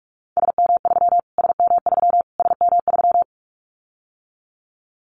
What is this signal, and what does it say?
Morse code, heard as a keyed tone, about 44 wpm, 700 Hz. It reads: SM3 SM3 SM3
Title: Callsign Prefixes for Top 25 DXCC - Repeated 3x - 45wpm